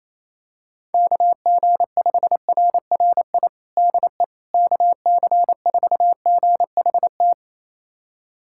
Morse code, heard as a keyed tone, about 28 wpm, 700 Hz. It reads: KG5RRI DE KC4GHT
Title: USA Callsign DE Callsign - 6 Characters - 28wpm